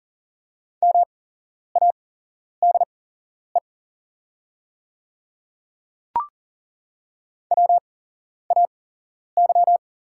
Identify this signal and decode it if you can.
Morse code, heard as a keyed tone, about 40 wpm, spaced as for 11 wpm, 690 Hz. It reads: MADE WAY
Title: 8x Character Spacing - Top 200 Words - 40wpm